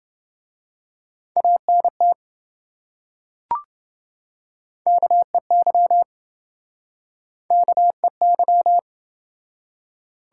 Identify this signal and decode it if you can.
Morse code, heard as a keyed tone, about 28 wpm, 700 Hz. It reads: ANT KEY KEY